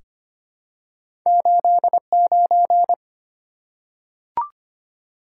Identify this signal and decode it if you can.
Morse code, heard as a keyed tone, about 25 wpm, 700 Hz. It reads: 89
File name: Two Letter-Number Combinations - 25wpm